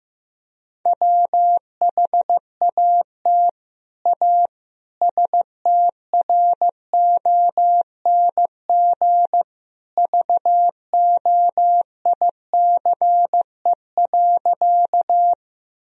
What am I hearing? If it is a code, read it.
Morse code, heard as a keyed tone, about 16 wpm, 700 Hz. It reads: WHAT A STRONG VOICE.
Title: Sentences from Top 1000 Words - Review Part 3 - 15wpm